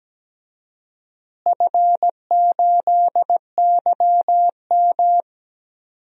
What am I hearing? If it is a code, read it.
Morse code, heard as a keyed tone, about 17 wpm, 700 Hz. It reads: F8YM
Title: DXCC-100 Call Signs - 4 Characters - Repeated 3x - 17wpm